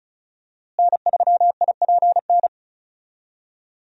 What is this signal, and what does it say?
Morse code, heard as a keyed tone, about 34 wpm, 700 Hz. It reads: N3IPN
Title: USA Call Signs - 5 Characters - Repeated 3x - 35wpm